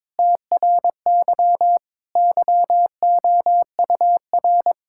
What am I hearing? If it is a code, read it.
Morse code, heard as a keyed tone, about 22 wpm, 700 Hz. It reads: TRY YOUR
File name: Sentences with 7-Word Phrases - 22wpm